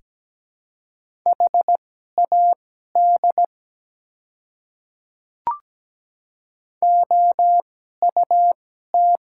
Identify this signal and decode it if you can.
Morse code, heard as a keyed tone, about 17 wpm, spaced as for 12 wpm, 700 Hz. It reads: HAD OUT
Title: 2x Character Spacing - Top 200 Words - 17wpm